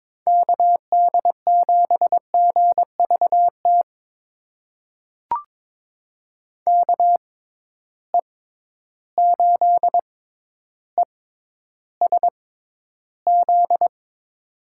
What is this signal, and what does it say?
Morse code, heard as a keyed tone, about 22 wpm, spaced as for 11 wpm, 700 Hz. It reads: KD7GVT KE8ESZ
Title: USA Call Signs - 6 Characters - 6x Character Spacing - 22wpm